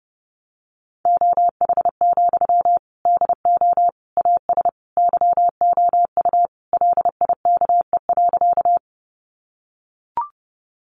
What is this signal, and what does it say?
Morse code, heard as a keyed tone, about 30 wpm, 700 Hz. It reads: OH, DO AS YOU LIKE.
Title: Sentences from Top 500 Words 30wpm